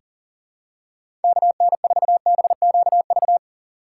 Morse code, heard as a keyed tone, about 40 wpm, 700 Hz. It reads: KN4BQV
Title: USA Call Signs - 2x Character Spacing - 40wpm